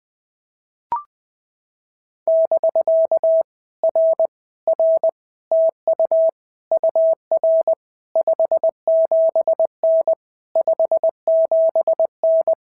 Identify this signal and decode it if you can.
Morse code, heard as a keyed tone, about 20 wpm, 650 Hz. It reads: <BK> R R TU UR 57N 57N
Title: POTA QSO Practice - 20wpm